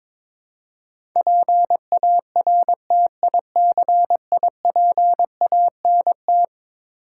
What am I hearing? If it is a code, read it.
Morse code, heard as a keyed tone, about 22 wpm, 700 Hz. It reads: PARTICIPANT
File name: practice / 22wpm / Eleven Letter Words - 22wpm